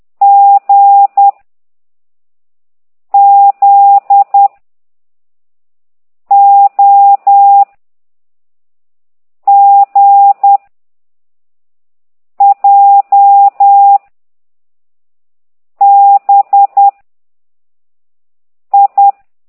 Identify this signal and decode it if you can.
Morse code, heard as a keyed tone, about 10 wpm, 800 Hz. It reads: G Z O G J B I